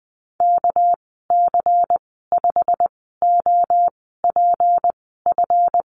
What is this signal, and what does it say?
Morse code, heard as a keyed tone, about 20 wpm, spaced as for 14 wpm, 700 Hz. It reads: KC5OPF
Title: 207 Lesson 33g - Letter j - Call Signs 20wpm